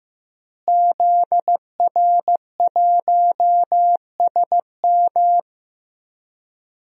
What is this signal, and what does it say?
Morse code, heard as a keyed tone, about 15 wpm, 700 Hz. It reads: ZR1SM
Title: DXCC-100 Call Signs - 5 Characters - Repeated 3x - 15wpm